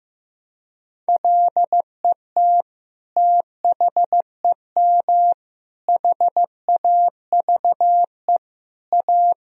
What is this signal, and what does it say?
Morse code, heard as a keyed tone, about 15 wpm, 700 Hz. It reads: LET THEM HAVE A
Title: Sentences from Top 1000 Words - Review Part 3 - 15wpm